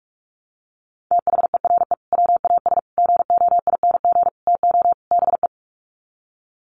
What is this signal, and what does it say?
Morse code, heard as a keyed tone, about 44 wpm, 700 Hz. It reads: THERE WAS GOING TO BE
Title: Sets of 5 Words Encore2 45wpm